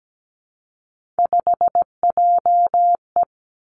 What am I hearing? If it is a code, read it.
Morse code, heard as a keyed tone, about 17 wpm, 700 Hz. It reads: 5JE
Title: Three Challenging Letter-Number Combinations 17wpm